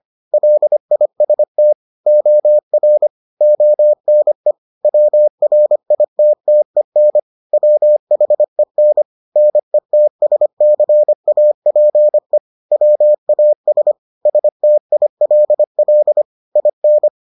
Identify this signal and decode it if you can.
Morse code, heard as a keyed tone, about 25 wpm, 600 Hz. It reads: LIST OR ONE WRITTEN WHEN NETSCAPE WAS STILL IN